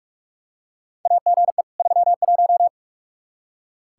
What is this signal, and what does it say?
Morse code, heard as a keyed tone, about 46 wpm, 700 Hz. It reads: AGE 31
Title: K1USN SST QSO Practice 45wpm